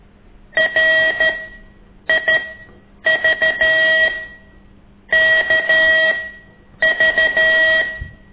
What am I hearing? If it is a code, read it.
Morse code, heard as a keyed tone, about 13 wpm, spaced as for 8 wpm, 1900 Hz. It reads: RIVKV
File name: ik1weg2 / CORSI CW